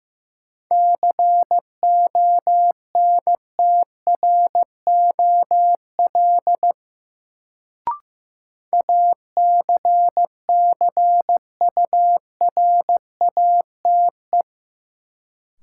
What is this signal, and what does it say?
Morse code, heard as a keyed tone, about 15 wpm, 700 Hz. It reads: CONTROL ACCURATE
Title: Sentences from Top 1200 Words - New Word Drill Alt - 15wpm